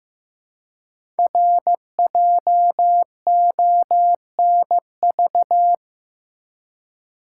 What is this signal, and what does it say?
Morse code, heard as a keyed tone, about 15 wpm, 700 Hz. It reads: RJONV
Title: Code Groups - 5 Letters - 15wpm